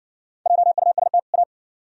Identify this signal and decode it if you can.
Morse code, heard as a keyed tone, about 56 wpm, 700 Hz. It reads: JUST A